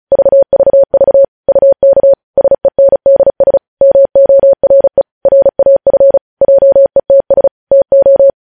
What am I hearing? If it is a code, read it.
Morse code, heard as a keyed tone, about 36 wpm, 550 Hz. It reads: VVV UK SENDS MORE RAF JETS TO
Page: news headlines in morse code at 35 wpm